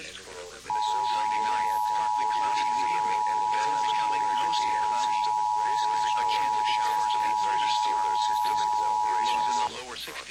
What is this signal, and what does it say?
Emergency Alert System (SAME) tones (the 853 and 960 Hz attention tone), with speech. It is heard from 0.6 to 9.7 s.